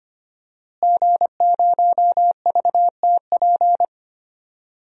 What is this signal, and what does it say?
Morse code, heard as a keyed tone, about 25 wpm, 700 Hz. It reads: G0VTP
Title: DXCC-100 Call Signs - 5 Characters - Repeated 3x - 25wpm